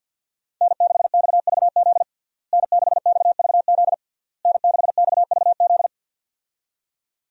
Morse code, heard as a keyed tone, about 50 wpm, 700 Hz. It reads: N6XVZ N6XVZ N6XVZ
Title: USA Call Signs - 5 Characters - Repeated 3x - 50wpm